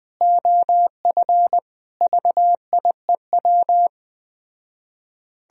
Morse code, heard as a keyed tone, about 20 wpm, 700 Hz. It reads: OF VIEW
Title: Speed-Racing - Sets of 5 Words 17wpm